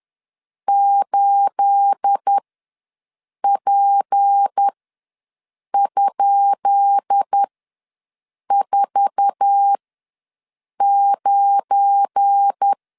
Morse code, heard as a keyed tone, about 10 wpm, spaced as for 6 wpm, 800 Hz. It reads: 8P?49
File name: LEZIONI / 10 SIGNS / 10B INT ws 10wpm 4134